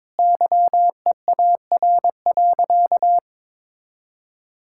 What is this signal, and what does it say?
Morse code, heard as a keyed tone, about 22 wpm, 700 Hz. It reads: YEAR.
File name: practice / 22wpm / Sentences with 8-Word Phrases - 22wpm